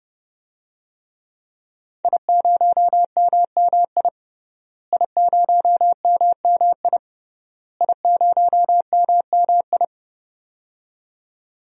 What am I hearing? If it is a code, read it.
Morse code, heard as a keyed tone, about 28 wpm, 700 Hz. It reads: I0MMI I0MMI I0MMI
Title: DXCC-100 Call Signs - 5 Characters - Repeated 3x - 30wpm